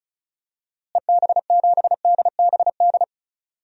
Morse code, heard as a keyed tone, about 36 wpm, 700 Hz. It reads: EB7DBD
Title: DXCC-100 Call Signs - 6 Characters - Repeated 3x - 35wpm